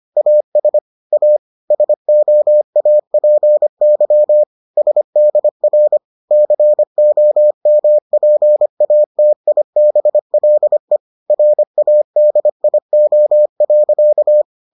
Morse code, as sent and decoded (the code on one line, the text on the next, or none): .- ... / .- / ... --- .- .--. -.-- / ... -.. .-. / -.-. --- -- .--. .- - .. -... .-.. / .-. .- -.. .. --- .-.-.-
AS A SOAPY SDR COMPATIBLE RADIO.